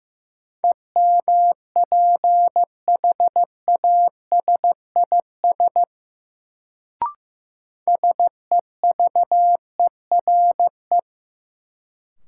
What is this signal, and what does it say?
Morse code, heard as a keyed tone, about 15 wpm, 700 Hz. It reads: EMPHASIS SEVERE
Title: Sentences from Top 1400 Words - New Word Drill - 15wpm